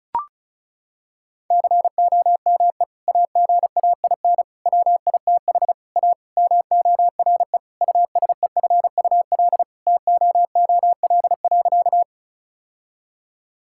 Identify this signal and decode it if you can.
Morse code, heard as a keyed tone, about 34 wpm, 700 Hz. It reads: COME AGAIN WITH A MORE USEFUL TOOL.
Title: Sentences from Top 1000 Words - Review Part 6 - 35wpm